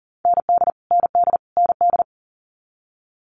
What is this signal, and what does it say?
Morse code, heard as a keyed tone, about 40 wpm, 700 Hz. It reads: ND ND ND
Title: Multi-letter Phonemes Repeated 3x 40wpm